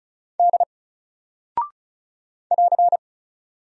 Morse code, heard as a keyed tone, about 36 wpm, 700 Hz. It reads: D <AR>
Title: Single Character (Letter:Number:Punctuation:Prosign) Alt 35wpm